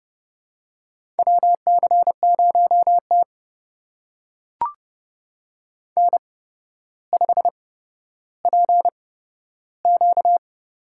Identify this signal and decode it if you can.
Morse code, heard as a keyed tone, about 30 wpm, spaced as for 13 wpm, 700 Hz. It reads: WC0T N5PQ
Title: USA Call Signs - 4 Characters - 8x Character Spacing - 30wpm